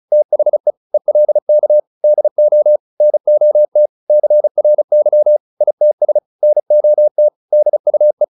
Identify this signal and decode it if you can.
Morse code, heard as a keyed tone, about 34 wpm, 600 Hz. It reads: THE ELK DO NOT CRY ITS NOT DUE